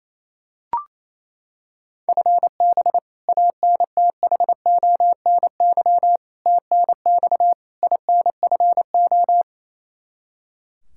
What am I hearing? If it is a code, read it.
Morse code, heard as a keyed tone, about 28 wpm, 700 Hz. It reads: FB ANTHONY TNX INFO
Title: K1USN SST QSO Practice 28wpm